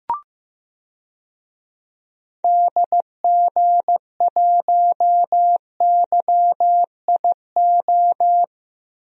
Morse code, heard as a keyed tone, about 15 wpm, 700 Hz. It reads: DG1YIO
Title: DXCC-100 Call Signs - 6 Characters - Repeated 2x - 15wpm